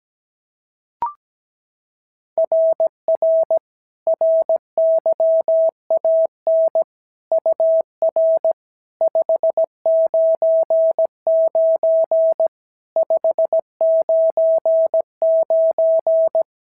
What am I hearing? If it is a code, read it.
Morse code, heard as a keyed tone, about 15 wpm, 650 Hz. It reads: RR RYAN UR 599 599
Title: SOTA QSO Practice 17wpm